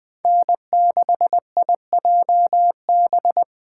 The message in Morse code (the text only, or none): N6IJB